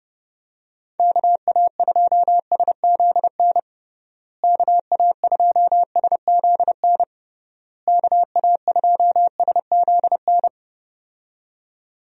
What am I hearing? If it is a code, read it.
Morse code, heard as a keyed tone, about 31 wpm, 700 Hz. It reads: KA2SZN KA2SZN KA2SZN
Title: USA Call Signs - 6 Characters - Repeated 3x - 30wpm